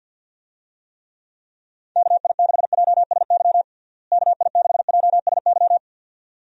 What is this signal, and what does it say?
Morse code, heard as a keyed tone, about 50 wpm, 700 Hz. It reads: KI6JSY KI6JSY
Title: USA Call Signs - 6 Characters - Repeated 2x - 50wpm